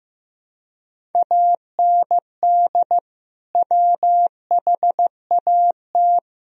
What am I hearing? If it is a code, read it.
Morse code, heard as a keyed tone, about 15 wpm, 700 Hz. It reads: AND WHAT
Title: Sentences from Top 1000 Words - Review Part 1 - 15wpm